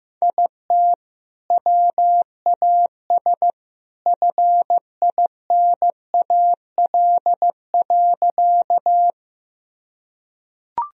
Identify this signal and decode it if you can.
Morse code, heard as a keyed tone, about 16 wpm, 700 Hz. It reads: IT WAS FINAL.
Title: Sentences from Top 1000 Words - Review Part 2 - 15wpm